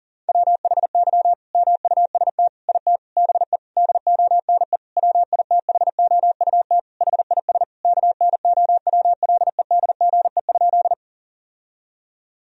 Morse code, heard as a keyed tone, about 40 wpm, 700 Hz. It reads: WHY MUST IT BE DONE WITHOUT HIS KNOWLEDGE?
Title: Sentences from Top 1000 Words - Review Part 3 - 40wpm